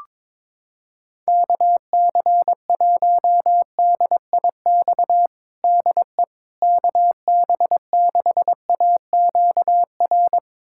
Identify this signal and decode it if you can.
Morse code, heard as a keyed tone, about 22 wpm, 700 Hz. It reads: KC1DIX DE KB6AQR
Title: USA Callsign DE Callsign 22wpm